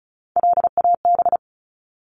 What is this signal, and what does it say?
Morse code, heard as a keyed tone, about 34 wpm, 700 Hz. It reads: LAB
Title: Top 1751-2000 Words 35wpm